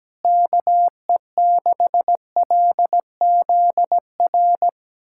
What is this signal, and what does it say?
Morse code, heard as a keyed tone, about 17 wpm, 700 Hz. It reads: KE6LZR
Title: USA Callsign DE Callsign - 6 Characters - 17wpm